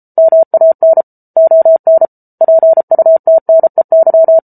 Morse code, heard as a keyed tone, about 34 wpm, 650 Hz. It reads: MAN ON PUTNEY